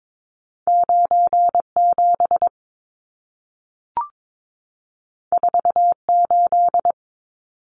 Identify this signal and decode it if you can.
Morse code, heard as a keyed tone, about 22 wpm, 700 Hz. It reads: 97 48
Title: Numbers - 2-Digit 22wpm